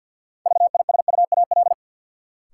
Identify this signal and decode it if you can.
Morse code, heard as a keyed tone, about 53 wpm, 700 Hz. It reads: VISUAL
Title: Sentences from Top 1000 Words - Review Part 6 - New Word Drill - 50wpm